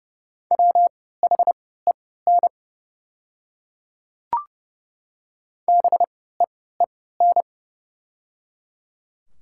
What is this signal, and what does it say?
Morse code, heard as a keyed tone, about 32 wpm, spaced as for 17 wpm, 690 Hz. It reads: WHEN BEEN
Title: 3x Character Spacing - Top 100 Words - 30wpm